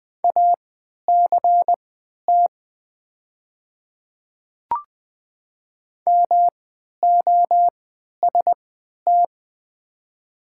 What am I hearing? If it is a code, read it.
Morse code, heard as a keyed tone, about 20 wpm, spaced as for 11 wpm, 690 Hz. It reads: ACT MOST
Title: 3x Character Spacing - Top 300 Words - 20wpm